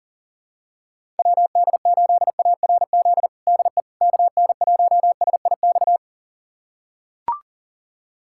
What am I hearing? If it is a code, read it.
Morse code, heard as a keyed tone, about 40 wpm, 700 Hz. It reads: WD8ARZ DE KN1SIX
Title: USA Callsign DE Callsign - 6 Characters - 40wpm